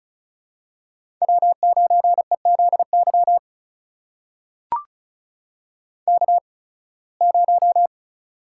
Morse code, heard as a keyed tone, about 36 wpm, spaced as for 23 wpm, 700 Hz. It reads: W9EZY K0
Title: USA Call Signs - 5 Characters - 8x Character Spacing - 35wpm